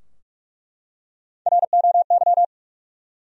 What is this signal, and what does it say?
Morse code, heard as a keyed tone, about 45 wpm, 700 Hz. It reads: ROY
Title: USA Common Names - 45wpm